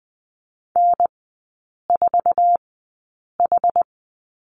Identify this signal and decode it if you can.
Morse code, heard as a keyed tone, about 20 wpm, spaced as for 8 wpm, 700 Hz. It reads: N4H
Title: 074 Lesson 13e - Letter L - Three Characters 20wpm